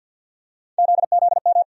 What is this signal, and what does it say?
Morse code, heard as a keyed tone, about 52 wpm, 690 Hz. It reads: ZZM
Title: USA Call Sign Postfixes - 3 Letters 50wpm